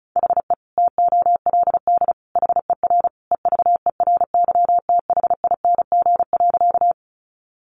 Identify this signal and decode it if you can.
Morse code, heard as a keyed tone, about 34 wpm, 700 Hz. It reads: HE TOLD HER EVERYTHING.
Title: Sentences from Top 700 Words 35wpm